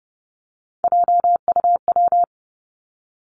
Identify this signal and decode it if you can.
Morse code, heard as a keyed tone, about 31 wpm, 700 Hz. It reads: JUW
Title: Three Challenging Letter-Number Combinations 30wpm